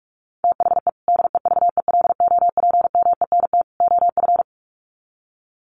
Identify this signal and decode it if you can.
Morse code, heard as a keyed tone, about 44 wpm, 700 Hz. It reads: THE DEVELOPMENT OF
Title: Sets of 3 Words 45wpm